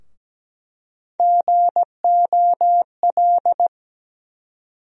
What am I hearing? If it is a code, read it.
Morse code, heard as a keyed tone, about 17 wpm, 700 Hz. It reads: GOL
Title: USA Call Sign Postfixes - 3 Letters 17wpm